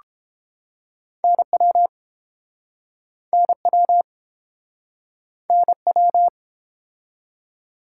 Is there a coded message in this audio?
Yes: Morse code from 1.2 to 6.3 s, heard as a keyed tone, about 30 wpm, 700 Hz. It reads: NW NW NW